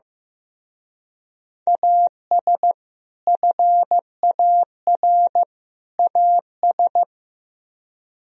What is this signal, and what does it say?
Morse code, heard as a keyed tone, about 18 wpm, 700 Hz. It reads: AS FAR AS
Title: Speed-Racing - Sets of 3 Words 15wpm